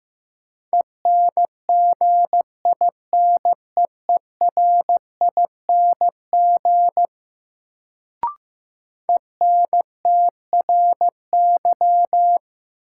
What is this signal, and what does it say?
Morse code, heard as a keyed tone, about 15 wpm, 700 Hz. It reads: ENGINEERING ENTRY
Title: Sentences from Top 1500 Words - New Word Drill - 15wpm